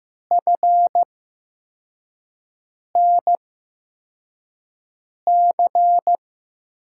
Morse code, heard as a keyed tone, about 15 wpm, spaced as for 4 wpm, 700 Hz. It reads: FNC
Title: USA Call Signs - 5 Characters - 8x Character Spacing - 15wpm